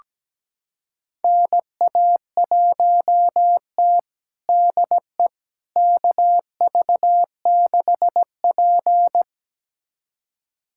Morse code, heard as a keyed tone, about 17 wpm, 700 Hz. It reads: NA1T DE KV6P
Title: USA Callsign DE Callsign - 4 Characters - 17wpm